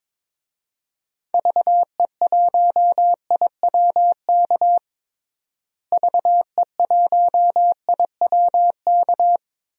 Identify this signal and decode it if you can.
Morse code, heard as a keyed tone, about 22 wpm, 700 Hz. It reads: VE1IWK VE1IWK
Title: DXCC-100 Call Signs - 6 Characters - Repeated 2x - 22wpm